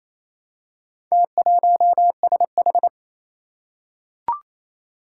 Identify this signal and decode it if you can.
Morse code, heard as a keyed tone, about 29 wpm, 700 Hz. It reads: T1SH
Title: Four Challenging Letter-Number Combinations 28wpm